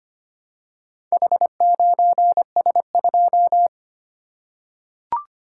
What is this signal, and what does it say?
Morse code, heard as a keyed tone, about 25 wpm, 690 Hz. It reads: H9S2
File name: practice / 25wpm / Four Letter-Number Combinations - 25wpm